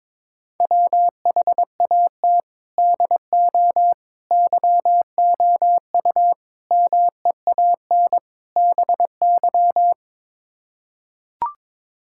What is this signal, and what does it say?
Morse code, heard as a keyed tone, about 22 wpm, 700 Hz. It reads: WHAT DO YOU MEAN BY
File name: practice / 22wpm / Build A Phrase - 5-Word Phrases - 22wpm